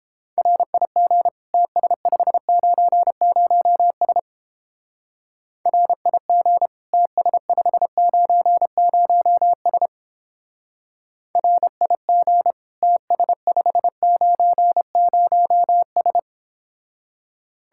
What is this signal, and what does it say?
Morse code, heard as a keyed tone, about 29 wpm, 700 Hz. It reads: RIG TS590S RIG TS590S RIG TS590S